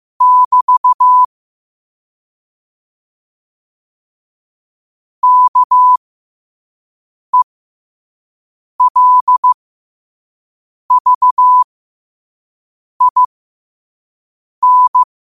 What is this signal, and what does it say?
Morse code, heard as a keyed tone, about 15 wpm, spaced as for 5 wpm, 1000 Hz. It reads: <BT> KELVIN
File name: qotdfiles / 5wpm